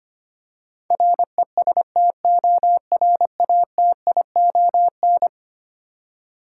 Morse code, heard as a keyed tone, about 25 wpm, 700 Hz. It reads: RESTORATION
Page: morse-code-ninja-practice-25wpm